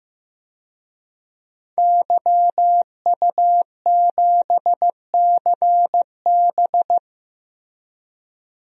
Morse code, heard as a keyed tone, about 15 wpm, 700 Hz. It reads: YU7CB
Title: DXCC-100 Call Signs - 5 Characters - 15wpm